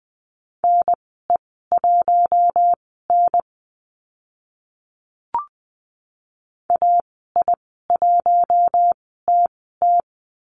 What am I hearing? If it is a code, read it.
Morse code, heard as a keyed tone, about 20 wpm, spaced as for 15 wpm, 700 Hz. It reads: NE1N AI1TT
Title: 042 Lesson 8g - Number 1 - Call Signs 20wpm